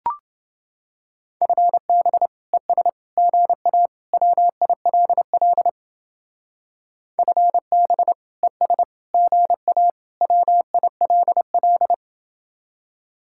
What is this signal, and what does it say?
Morse code, heard as a keyed tone, about 30 wpm, 700 Hz. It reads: FB ES GA WILL FB ES GA WILL